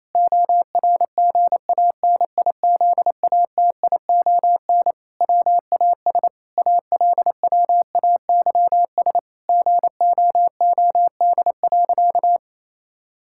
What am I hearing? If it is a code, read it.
Morse code, heard as a keyed tone, about 28 wpm, 700 Hz. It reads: ORGANIZATION WAS ALWAYS GOOD.
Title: Sentences from Top 1000 Words - Review Part 5 - 28wpm